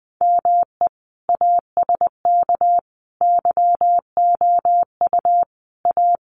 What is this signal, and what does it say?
Morse code, heard as a keyed tone, about 20 wpm, 700 Hz. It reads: ME ASK YOU A
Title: Sets of 5 Words Encore2 20wpm